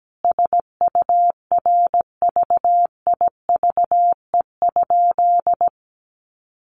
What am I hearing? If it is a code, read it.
Morse code, heard as a keyed tone, about 17 wpm, 700 Hz. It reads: SURVIVE?
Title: Sentences from Top 1300 Words 17wpm